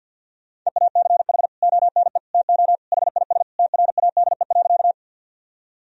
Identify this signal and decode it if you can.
Morse code, heard as a keyed tone, about 50 wpm, 700 Hz. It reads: EACH ONE TO HIS TRADE.
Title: Sentences from Top 1000 Words - Review Part 4 - 50wpm